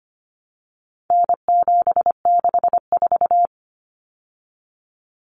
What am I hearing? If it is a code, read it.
Morse code, heard as a keyed tone, about 25 wpm, 700 Hz. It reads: N764
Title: Common Cut Numbers - 4-Digit 25wpm